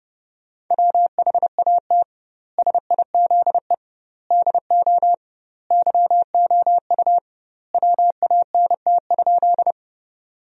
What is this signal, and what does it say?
Morse code, heard as a keyed tone, about 32 wpm, 700 Hz. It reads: WHAT SIZE DO YOU WANT?
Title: 2x Word-Spacing - Sentences from Top 400 Words - 30wpm